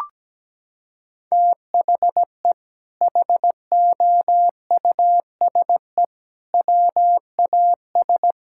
Morse code, heard as a keyed tone, about 17 wpm, 700 Hz. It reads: THE HOUSE WAS
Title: Sentences from Top 1000 Words - Review Part 6 - 17wpm